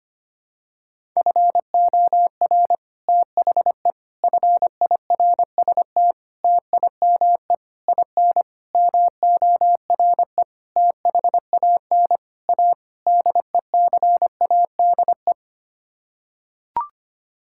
Morse code, heard as a keyed tone, about 25 wpm, 700 Hz. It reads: FOR THE FIRST TIME IN MORE THAN A DECADE
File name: Sets of 9 Words - 25wpm